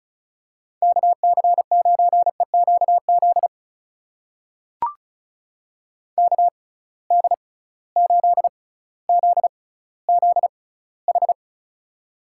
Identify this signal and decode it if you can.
Morse code, heard as a keyed tone, about 36 wpm, spaced as for 18 wpm, 700 Hz. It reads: KC9EQZ KD8ZZH
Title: USA Call Signs - 6 Characters - 6x Character Spacing - 35wpm